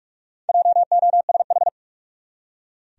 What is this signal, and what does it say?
Morse code, heard as a keyed tone, about 44 wpm, 700 Hz. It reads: JOSH
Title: USA Common Names - Max 4 Letters - 45wpm